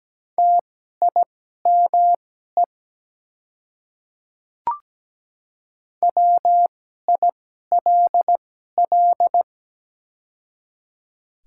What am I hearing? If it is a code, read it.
Morse code, heard as a keyed tone, about 17 wpm, spaced as for 12 wpm, 700 Hz. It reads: TIME WILL
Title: 2x Character Spacing - Top 100 Words - 17wpm